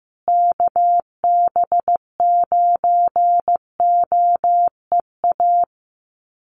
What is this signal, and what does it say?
Morse code, heard as a keyed tone, about 15 wpm, 700 Hz. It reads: KB9OEA
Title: USA Call Signs Repeated 2x 15wpm